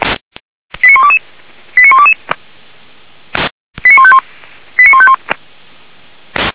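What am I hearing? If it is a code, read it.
Five-tone selective call (ZVEI), heard as quick runs of five tones, tones of about 70 ms. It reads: 89122 89122 89151 89151
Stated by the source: Received by SDR as a ZVEI five-tone selective call.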